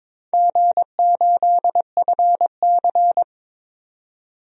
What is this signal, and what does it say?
Morse code, heard as a keyed tone, about 23 wpm, 700 Hz. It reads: G8FC
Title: DXCC-100 Call Signs - 4 Characters - 22wpm